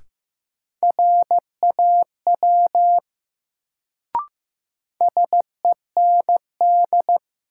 Morse code, heard as a keyed tone, about 15 wpm, 700 Hz. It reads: RAW SEND
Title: Sentences from Top 1500 Words - New Word Drill - 15wpm